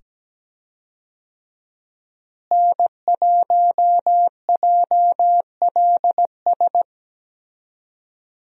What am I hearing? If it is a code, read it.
Morse code, heard as a keyed tone, about 17 wpm, 700 Hz. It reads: N1JLS
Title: USA Call Signs - 5 Characters - Repeated 2x - 17wpm